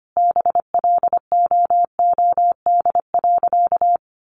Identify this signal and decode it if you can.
Morse code, heard as a keyed tone, about 25 wpm, 700 Hz. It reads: BLOOD.
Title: Sentences from Top 1100 Words 25wpm